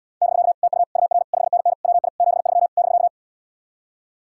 Morse code, heard as a keyed tone, about 80 wpm, 700 Hz. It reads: <BK> TU UR 5NN WI 73 <BK>